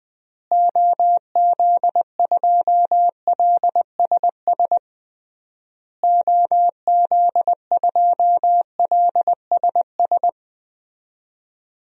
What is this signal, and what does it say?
Morse code, heard as a keyed tone, about 21 wpm, 700 Hz. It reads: OZ2LSS OZ2LSS
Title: DXCC-100 Call Signs - 6 Characters - Repeated 3x - 20wpm